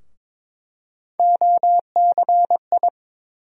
Morse code, heard as a keyed tone, about 22 wpm, 700 Hz. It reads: OCI
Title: USA Call Sign Postfixes - 3 Letters 22wpm